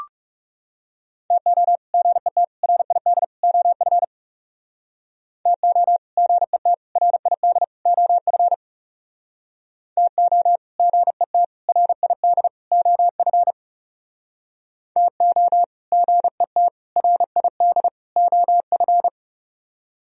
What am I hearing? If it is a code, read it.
Morse code, heard as a keyed tone, about 39 wpm, 700 Hz. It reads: TO GET RID OF TO GET RID OF TO GET RID OF TO GET RID OF